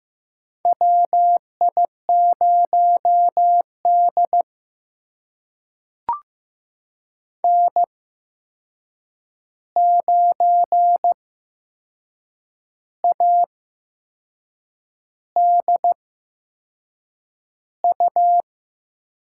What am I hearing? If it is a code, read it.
Morse code, heard as a keyed tone, about 15 wpm, spaced as for 6 wpm, 700 Hz. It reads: WI0D N9ADU
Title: USA Call Signs - 8x Character Spacing - 15wpm